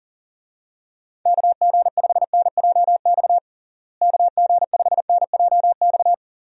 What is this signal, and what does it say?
Morse code, heard as a keyed tone, about 40 wpm, 700 Hz. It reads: KG5NJX KG5NJX
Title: USA Call Signs - 6 Characters - Repeated 2x - 40wpm